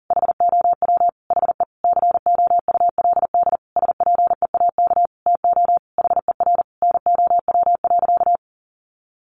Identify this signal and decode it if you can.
Morse code, heard as a keyed tone, about 40 wpm, 700 Hz. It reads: HOW HE COULD SPEAK TO HER NOW.
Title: Sentences from Top 1400 Words 40wpm